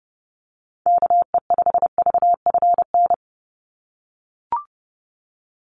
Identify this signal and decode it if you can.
Morse code, heard as a keyed tone, about 30 wpm, 700 Hz. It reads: KE5VFN
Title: USA Call Signs 30wpm